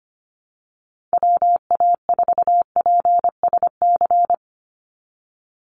Morse code, heard as a keyed tone, about 25 wpm, 700 Hz. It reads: WA4PSC
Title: Call Signs - 25wpm